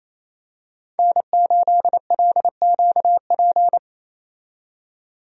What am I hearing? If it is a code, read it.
Morse code, heard as a keyed tone, about 28 wpm, 700 Hz. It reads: N8LQP
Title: USA Call Signs - 5 Characters - 28wpm